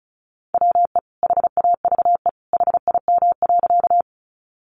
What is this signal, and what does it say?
Morse code, heard as a keyed tone, about 34 wpm, 700 Hz. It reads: WE HAVE HIM.
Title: Sentences from Top 100 Words 35wpm